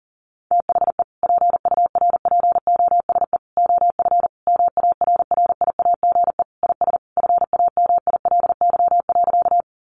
Morse code, heard as a keyed tone, about 40 wpm, 690 Hz. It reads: THE PURPOSE OF MARRIAGE IS FAMILY.
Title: Sentences from Top 800 Words 40wpm